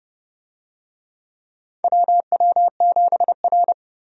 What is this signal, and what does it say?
Morse code, heard as a keyed tone, about 31 wpm, 700 Hz. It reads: WW7R
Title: USA Call Signs - 4 Characters - 30wpm